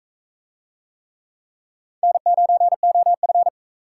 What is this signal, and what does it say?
Morse code, heard as a keyed tone, about 42 wpm, 700 Hz. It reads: N9OF